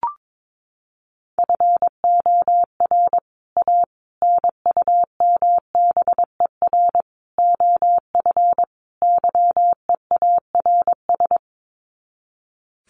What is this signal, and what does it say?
Morse code, heard as a keyed tone, about 22 wpm, 700 Hz. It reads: FOR A NUMBER OF YEARS
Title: Sets of 5 Words Encore 22wpm